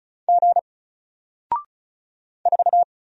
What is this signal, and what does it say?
Morse code, heard as a keyed tone, about 36 wpm, 700 Hz. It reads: G 4
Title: Single Letter-Number 35wpm